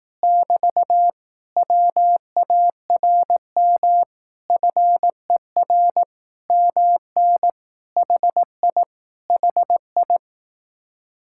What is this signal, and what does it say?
Morse code, heard as a keyed tone, about 18 wpm, 700 Hz. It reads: <BT> WARM FER MN HI HI
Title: Speed-Racing Common QSO Phrases Encore - 15wpm